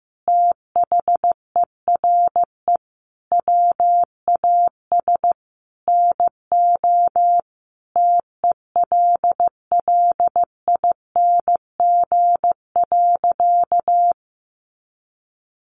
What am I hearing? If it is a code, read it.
Morse code, heard as a keyed tone, about 15 wpm, 700 Hz. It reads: THERE WAS NO TELLING.
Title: A Sound Of Thunder 15wpm